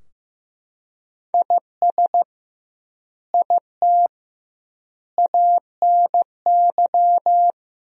Morse code, heard as a keyed tone, about 15 wpm, 700 Hz. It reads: IS IT ANY
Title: 2x Word-Spacing - Sentences from Top 200 Words - 15wpm